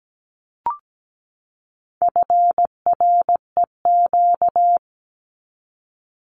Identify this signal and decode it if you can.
Morse code, heard as a keyed tone, about 17 wpm, 700 Hz. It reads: FREQ
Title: Abbreviations 17wpm